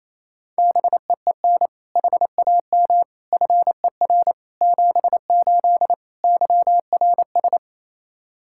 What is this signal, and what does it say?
Morse code, heard as a keyed tone, about 28 wpm, 700 Hz. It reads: BEEN HAM FER 78 YRS
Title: Practice QSO - Exchange 3 - More Info - 28wpm